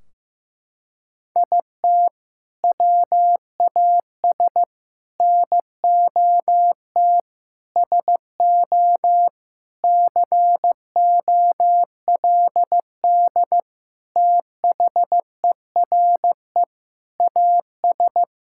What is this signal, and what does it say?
Morse code, heard as a keyed tone, about 15 wpm, 700 Hz. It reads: IT WAS NOT SO COLD THERE AS
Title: Sentences from Top 1600 Words 15wpm